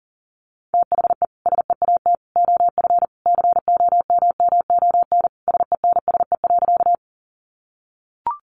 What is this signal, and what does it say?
Morse code, heard as a keyed tone, about 40 wpm, 700 Hz. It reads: THE SEAT OF COMMON SENSE.
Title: Sentences from Top 1500 Words 40wpm